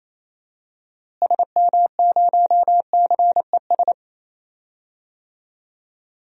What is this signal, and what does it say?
Morse code, heard as a keyed tone, about 28 wpm, 700 Hz. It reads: SM0CES
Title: DXCC-100 Call Signs - 6 Characters - Repeated 2x - 28wpm